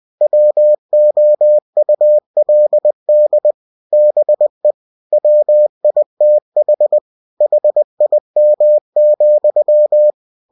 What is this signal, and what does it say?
Morse code, heard as a keyed tone, about 20 wpm, 600 Hz. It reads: WOULD BE WITH HIM,
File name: cw-books / winnie-the-pooh / winnie-20wpm-page-0041